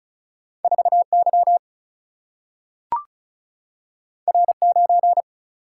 Morse code, heard as a keyed tone, about 36 wpm, 700 Hz. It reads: 4 Y R 9
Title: Single-Letter Single-Number 35wpm